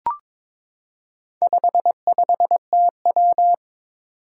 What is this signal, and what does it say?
Morse code, heard as a keyed tone, about 22 wpm, 700 Hz. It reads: 55TW
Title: Four Challenging Letter-Number Combinations 22wpm